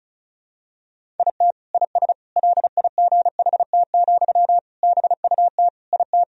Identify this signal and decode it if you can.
Morse code, heard as a keyed tone, about 36 wpm, 700 Hz. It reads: IT IS LIGHT, BUT IT
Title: Sentences from Top 1000 Words - Review Part 1 - 35wpm